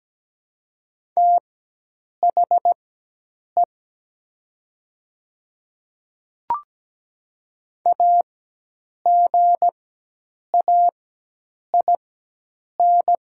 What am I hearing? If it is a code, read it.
Morse code, heard as a keyed tone, about 17 wpm, spaced as for 8 wpm, 700 Hz. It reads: THE AGAIN
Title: 4x Character Spacing - Top 200 Words - 17wpm